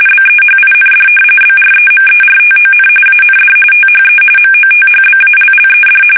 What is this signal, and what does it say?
RTTY, about 45.45 baud. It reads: BEHIND BREMERTON AND THE MOTHBALLED
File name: fsk850 / battleshipnj